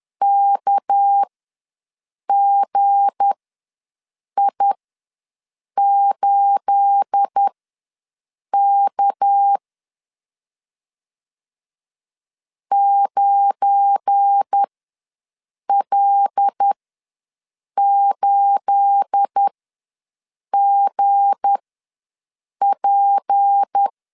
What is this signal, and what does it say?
Morse code, heard as a keyed tone, about 10 wpm, spaced as for 6 wpm, 800 Hz. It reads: KGI8K 9L8GP
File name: LEZIONI / 9 28 ws 10wpm 4124